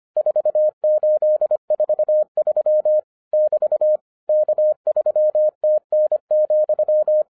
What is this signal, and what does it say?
Morse code, heard as a keyed tone, about 24 wpm, 610 Hz. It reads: 4843 <BT> K3TN,